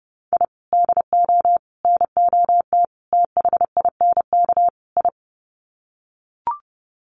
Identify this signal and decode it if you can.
Morse code, heard as a keyed tone, about 30 wpm, 700 Hz. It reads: I DO NOT THINK I
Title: Sets of 5 Words 30wpm